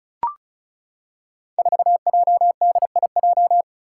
Morse code, heard as a keyed tone, about 36 wpm, 700 Hz. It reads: 4JDIJ
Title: Five Challenging Letter-Number Combinations 35wpm